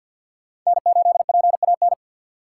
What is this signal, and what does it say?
Morse code, heard as a keyed tone, about 46 wpm, 700 Hz. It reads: N8PAN
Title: USA Call Signs - 3x Character Spacing - 50wpm